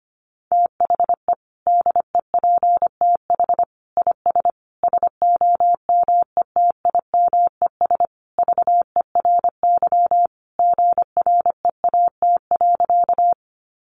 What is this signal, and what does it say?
Morse code, heard as a keyed tone, about 25 wpm, 700 Hz. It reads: THE DEPTH IS SOMETIMES VERY GREAT.
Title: Sentences from Top 1200 Words 25wpm